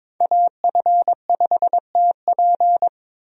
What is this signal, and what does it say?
Morse code, heard as a keyed tone, about 22 wpm, 700 Hz. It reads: AF5TP
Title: USA Call Signs - 5 Characters - 22wpm